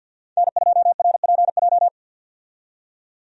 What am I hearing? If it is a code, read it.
Morse code, heard as a keyed tone, about 51 wpm, 700 Hz. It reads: N2RPJ
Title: USA Call Signs - 5 Characters - 50wpm